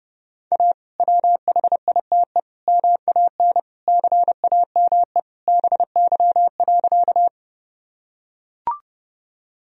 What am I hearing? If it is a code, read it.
Morse code, heard as a keyed tone, about 30 wpm, 700 Hz. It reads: A WHITE MAN CAME BY.
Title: Sentences from Top 1000 Words - Review Part 1 - 30wpm